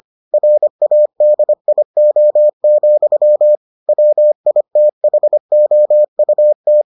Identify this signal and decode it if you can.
Morse code, heard as a keyed tone, about 25 wpm, 600 Hz. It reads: RADIO, WITHOUT